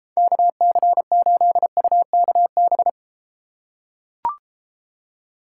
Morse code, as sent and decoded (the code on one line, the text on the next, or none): -.- -.-. ---.. ..- -.- -...
KC8UKB